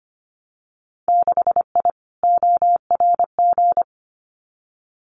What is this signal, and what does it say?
Morse code, heard as a keyed tone, about 25 wpm, 700 Hz. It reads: 6I ORG
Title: Winter Field Day Exchanges Encore 25wpm